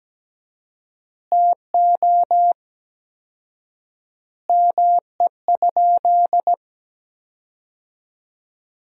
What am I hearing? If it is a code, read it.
Morse code, heard as a keyed tone, about 17 wpm, 700 Hz. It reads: TO ME?
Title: 4x Word-Spacing - Sentences from Top 500 Words - 17wpm